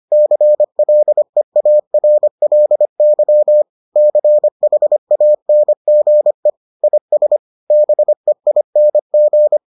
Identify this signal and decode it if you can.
Morse code, heard as a keyed tone, about 25 wpm, 600 Hz. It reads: CLEARLY CHANGE IS BEING